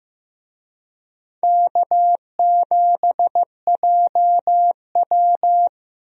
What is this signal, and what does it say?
Morse code, heard as a keyed tone, about 15 wpm, 700 Hz. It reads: K7JW
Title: USA Call Signs - 4 Characters - 15wpm